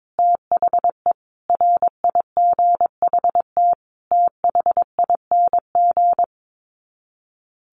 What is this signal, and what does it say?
Morse code, heard as a keyed tone, about 22 wpm, 700 Hz. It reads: THE RIGHT THING
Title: Sets of 5 Words 22wpm